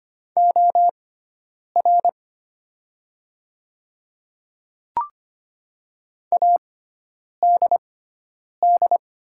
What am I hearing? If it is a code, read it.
Morse code, heard as a keyed tone, about 25 wpm, spaced as for 9 wpm, 700 Hz. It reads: OR ADD
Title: 6x Character Spacing - Top 200 Words - 25wpm